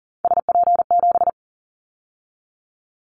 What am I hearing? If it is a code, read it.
Morse code, heard as a keyed tone, about 40 wpm, 700 Hz. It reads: SP7
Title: Callsign Prefixes for Top 25 DXCC - 40wpm